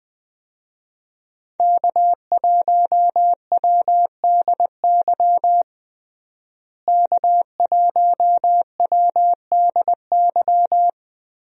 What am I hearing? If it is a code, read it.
Morse code, heard as a keyed tone, about 21 wpm, 700 Hz. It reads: K1WDY K1WDY
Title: USA Call Signs - 5 Characters - Repeated 2x - 20wpm